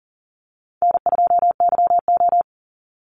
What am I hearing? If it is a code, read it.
Morse code, heard as a keyed tone, about 40 wpm, 700 Hz. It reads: N2YO
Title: Easy Call Signs Repeated 2x - 40wpm